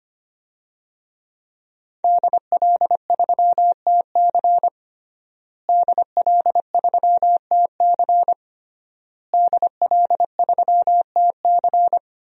Morse code, heard as a keyed tone, about 25 wpm, 700 Hz. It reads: DL3TC DL3TC DL3TC
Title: DXCC-100 Call Signs - 5 Characters - Repeated 3x - 25wpm